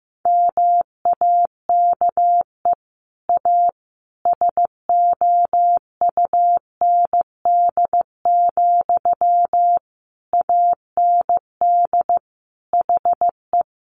Morse code, heard as a keyed tone, about 15 wpm, 700 Hz. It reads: MAKE A SOUND, AND HE
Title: Sentences from Top 1400 Words 15wpm